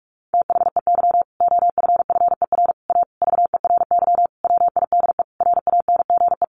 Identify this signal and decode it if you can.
Morse code, heard as a keyed tone, about 44 wpm, 700 Hz. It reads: THEY OFFER A VERY WIDE RANGE
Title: Sentences from Top 900 Words 45wpm